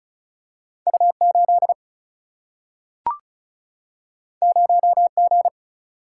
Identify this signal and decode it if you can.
Morse code, heard as a keyed tone, about 34 wpm, 700 Hz. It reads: U8 0G
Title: Single-Letter Single-Number 35wpm